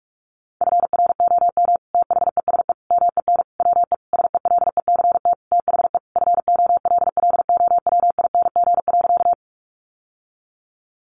Morse code, heard as a keyed tone, about 44 wpm, 700 Hz. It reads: FROM THESE MEN WE SELECT THE FOLLOWING.
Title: Sentences from Top 1300 Words 45wpm